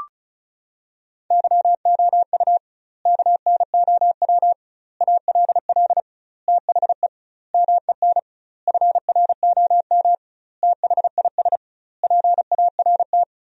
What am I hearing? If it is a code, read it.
Morse code, heard as a keyed tone, about 36 wpm, 700 Hz. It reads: YOU KNOW ALL THE MEN FROM THIS PART
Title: 2x Word-Spacing - Sentences from Top 300 Words - 35wpm